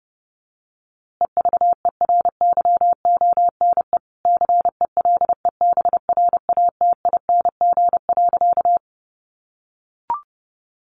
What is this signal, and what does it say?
Morse code, heard as a keyed tone, about 30 wpm, 700 Hz. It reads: EVERYONE CELEBRATING.
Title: A Sound Of Thunder 30wpm